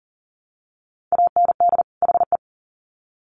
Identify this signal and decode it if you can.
Morse code, heard as a keyed tone, about 40 wpm, 700 Hz. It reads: AND HE
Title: Sets of 2 Words 40wpm